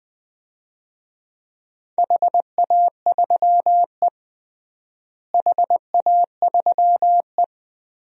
Morse code, heard as a keyed tone, about 20 wpm, 700 Hz. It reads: HA3E HA3E
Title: DXCC-100 Call Signs - 4 Characters - Repeated 2x - 20wpm